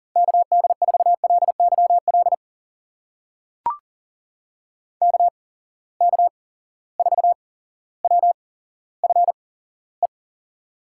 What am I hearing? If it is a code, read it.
Morse code, heard as a keyed tone, about 39 wpm, spaced as for 17 wpm, 700 Hz. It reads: KD4LYL KK4WFE
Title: USA Call Signs - 8x Character Spacing - 40wpm